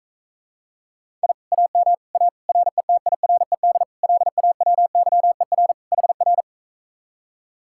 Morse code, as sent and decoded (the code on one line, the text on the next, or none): .. / .- -- / .- / .-. - .. .-. -.. / .-.. .- .-- -.-- . .-. / .... .-.
I AM A RETIRED LAWYER HR